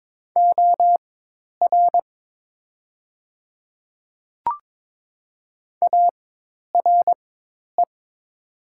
Morse code, heard as a keyed tone, about 22 wpm, spaced as for 10 wpm, 700 Hz. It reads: OR ARE
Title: 4x Character Spacing - Top 100 Words - 22wpm